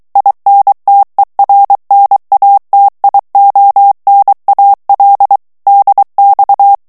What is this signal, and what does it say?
Morse code, heard as a keyed tone, about 23 wpm, 800 Hz. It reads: INTERNATIONAL DX
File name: ITALIEN texte clair 2 vitesse 22 mots minute